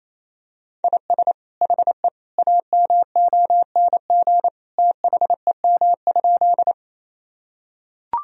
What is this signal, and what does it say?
Morse code, heard as a keyed tone, about 28 wpm, 700 Hz. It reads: IS HE AMONG THEM?
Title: Sentences from Top 1000 Words - Review Part 3 - 28wpm